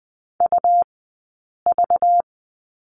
Morse code, heard as a keyed tone, about 20 wpm, spaced as for 8 wpm, 700 Hz. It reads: UV
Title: 197 Lesson 32d - Letter k - Two Characters 20wpm